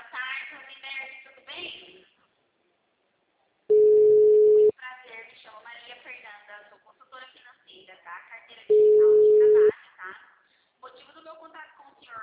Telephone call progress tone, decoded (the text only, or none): ringback tone